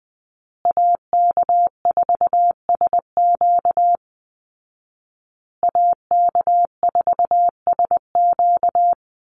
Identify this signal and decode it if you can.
Morse code, heard as a keyed tone, about 19 wpm, 700 Hz. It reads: AK4SQ AK4SQ
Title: Call Signs Repeated 2x - 20wpm